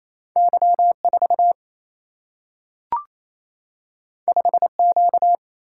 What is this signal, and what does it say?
Morse code, heard as a keyed tone, about 28 wpm, 700 Hz. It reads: Y 4 5 Q
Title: Single-Letter Single-Number 28wpm